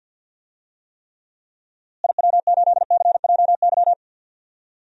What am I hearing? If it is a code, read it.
Morse code, heard as a keyed tone, about 47 wpm, 700 Hz. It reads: IW8CJY
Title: DXCC-100 Call Signs - 6 Characters - Repeated 2x - 50wpm